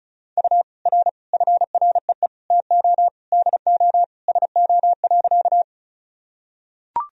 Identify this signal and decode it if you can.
Morse code, heard as a keyed tone, about 36 wpm, 700 Hz. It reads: U R FREE TO DO SO.
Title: Sentences from Top 500 Words and Abbreviations 35wpm